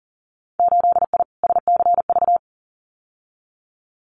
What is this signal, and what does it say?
Morse code, heard as a keyed tone, about 38 wpm, 700 Hz. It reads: 8I SCV
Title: Winter Field Day Exchanges Encore 40wpm